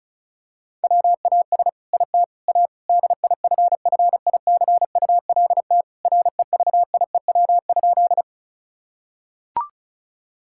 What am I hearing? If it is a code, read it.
Morse code, heard as a keyed tone, about 36 wpm, 700 Hz. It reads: WAS IT A DIFFICULT REVIEW?
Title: Sentences from Top 1000 Words - Review Part 6 - 35wpm